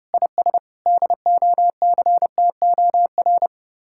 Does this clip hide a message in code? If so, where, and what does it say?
Yes: Morse code from 0.1 to 3.5 s, heard as a keyed tone, about 33 wpm, 700 Hz. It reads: IS DOCTOR